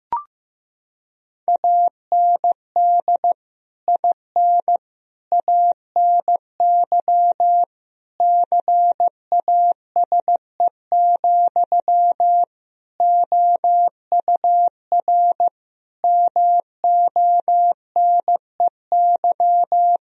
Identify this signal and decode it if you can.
Morse code, heard as a keyed tone, about 15 wpm, 700 Hz. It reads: AND IN ANY CASE, OUR MONEY
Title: Sentences from Top 1000 Words - Review Part 1 - 15wpm